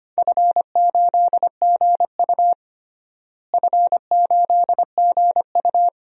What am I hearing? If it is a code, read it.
Morse code, heard as a keyed tone, about 25 wpm, 700 Hz. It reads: F8GU F8GU
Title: DXCC-100 Call Signs - 4 Characters - Repeated 2x - 25wpm